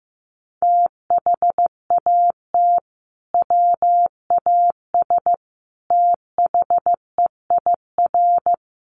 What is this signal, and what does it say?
Morse code, heard as a keyed tone, about 15 wpm, 700 Hz. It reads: THAT WAS THEIR
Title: Sentences from Top 1200 Words 15wpm